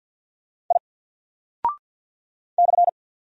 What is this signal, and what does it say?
Morse code, heard as a keyed tone, about 48 wpm, 700 Hz. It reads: I /
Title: Single Character (Letter:Number:Punctuation:Prosign) Alt 50wpm